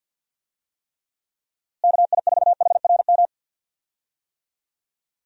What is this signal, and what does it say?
Morse code, heard as a keyed tone, about 46 wpm, 700 Hz. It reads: KI4HRM
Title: USA Call Signs - 6 Characters - Repeated 3x - 50wpm